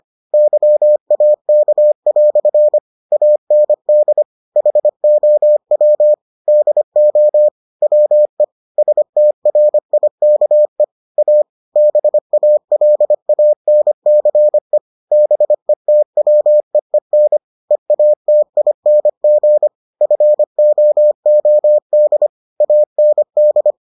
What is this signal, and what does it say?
Morse code, heard as a keyed tone, about 25 wpm, 600 Hz. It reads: YAK" AND HOW DO WE STRIKE A BALANCE BETWEEN EATING FOOD AND